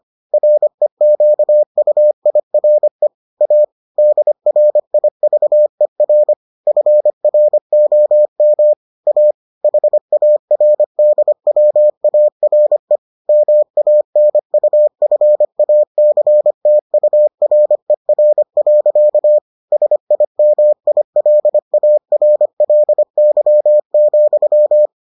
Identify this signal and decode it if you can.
Morse code, heard as a keyed tone, about 25 wpm, 600 Hz. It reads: REQUIRE A DRIVER FROM A HARDWARE MANUFACTURER. SIMILARLY,